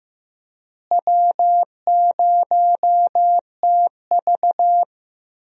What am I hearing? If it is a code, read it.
Morse code, heard as a keyed tone, about 15 wpm, 700 Hz. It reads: W0TV
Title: USA Call Signs - 4 Characters - Repeated 3x - 15wpm